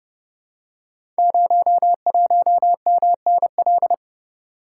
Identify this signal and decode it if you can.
Morse code, heard as a keyed tone, about 30 wpm, 700 Hz. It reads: 01MNL
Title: Code Groups - 5 Letter:Numbers - 30wpm